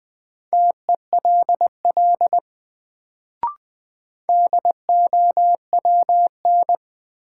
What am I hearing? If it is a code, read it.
Morse code, heard as a keyed tone, about 20 wpm, 700 Hz. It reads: TELL DOWN
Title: Sets of 2 Words - New Word Drill - 20wpm